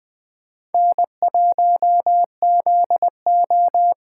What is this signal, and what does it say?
Morse code, heard as a keyed tone, about 20 wpm, 700 Hz. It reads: N1ZO
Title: Speed Racing - USA 1x2 Call Signs 17wpm